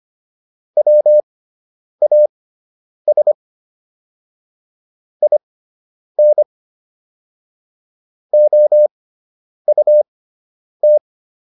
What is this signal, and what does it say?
Morse code, heard as a keyed tone, about 25 wpm, spaced as for 9 wpm, 600 Hz. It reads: WAS IN OUT